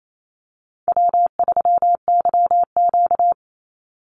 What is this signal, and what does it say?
Morse code, heard as a keyed tone, about 28 wpm, 700 Hz. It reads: W3YQ
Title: Call Signs - 28wpm